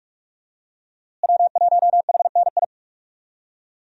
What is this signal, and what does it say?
Morse code, heard as a keyed tone, about 46 wpm, 700 Hz. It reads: W1HNI
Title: USA Call Signs - 5 Characters - Repeated 2x - 45wpm